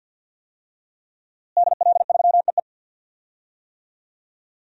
Morse code, heard as a keyed tone, about 48 wpm, 700 Hz. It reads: DL3EE
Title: DXCC-100 Call Signs - 5 Characters - Repeated 3x - 50wpm